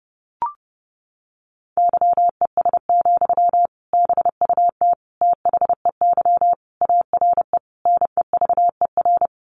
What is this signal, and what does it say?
Morse code, heard as a keyed tone, about 28 wpm, 700 Hz. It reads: YES, BUT THEY ARE NEVER
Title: Sentences from Top 600 Words 30wpm